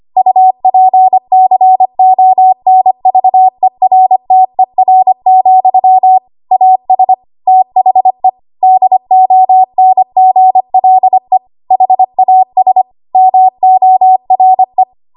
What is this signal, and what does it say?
Morse code, heard as a keyed tone, about 25 wpm, 750 Hz. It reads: UPCONVERTER, AS THE DONGLE HAS MORE
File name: Archive / 25 WPM / 25WPM